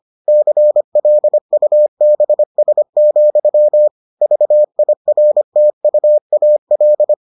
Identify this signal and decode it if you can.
Morse code, heard as a keyed tone, about 25 wpm, 600 Hz. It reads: CLUBS, VIRTUAL